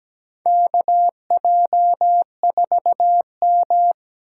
Morse code, heard as a keyed tone, about 17 wpm, 700 Hz. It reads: KJ4M
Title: USA Call Signs - 4 Characters - 3x Character Spacing - 17wpm